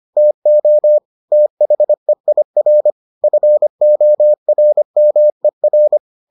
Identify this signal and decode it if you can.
Morse code, heard as a keyed tone, about 25 wpm, 600 Hz. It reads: TO THEIR FORMER